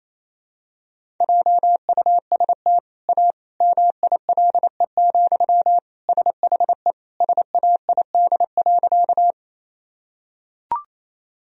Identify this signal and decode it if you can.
Morse code, heard as a keyed tone, about 28 wpm, 700 Hz. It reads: JUST A MILE, SHE SAID.
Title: Sentences from Top 1000 Words - Review Part 1 - 28wpm